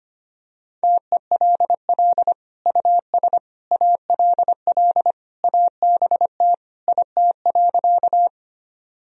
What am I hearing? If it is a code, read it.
Morse code, heard as a keyed tone, about 25 wpm, 700 Hz. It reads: TELL US ALL ABT IT.
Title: Sentences from Top 200 Words and Abbreviations 25wpm